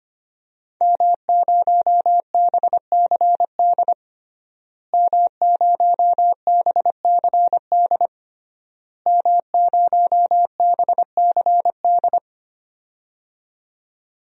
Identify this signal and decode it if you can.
Morse code, heard as a keyed tone, about 25 wpm, 700 Hz. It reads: M0BCD M0BCD M0BCD
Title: DXCC-100 Call Signs - 5 Characters - Repeated 3x - 25wpm